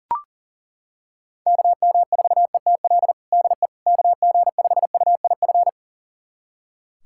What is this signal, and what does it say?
Morse code, heard as a keyed tone, about 40 wpm, 700 Hz. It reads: KM4ETL DE KG5UIF
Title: USA Callsign DE Callsign - 6 Characters - 40wpm